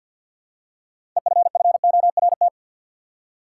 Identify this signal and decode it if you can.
Morse code, heard as a keyed tone, about 53 wpm, 700 Hz. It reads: EFFORT
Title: Six Letter Words - 50wpm